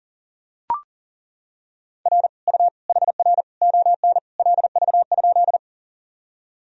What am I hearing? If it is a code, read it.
Morse code, heard as a keyed tone, about 40 wpm, 700 Hz. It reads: R U HR ON LV?
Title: Sentences from Top 400 Words and Abbreviations 40wpm